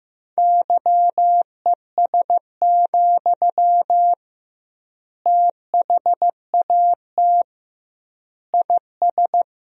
Morse code, heard as a keyed tone, about 15 wpm, 700 Hz. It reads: YES, THAT IS
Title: 2x Word-Spacing - Sentences from Top 500 Words - 15wpm